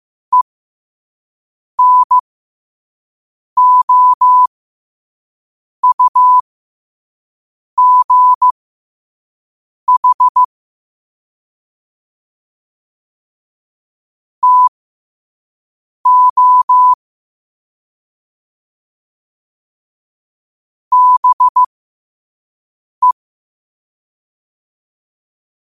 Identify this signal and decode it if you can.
Morse code, heard as a keyed tone, about 15 wpm, spaced as for 5 wpm, 1000 Hz. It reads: ENOUGH TO BE
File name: qotdfiles / 5wpm